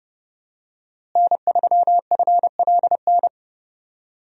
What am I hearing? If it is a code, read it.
Morse code, heard as a keyed tone, about 32 wpm, 700 Hz. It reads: N3FLN
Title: USA Call Signs - 5 Characters - Repeated 2x - 30wpm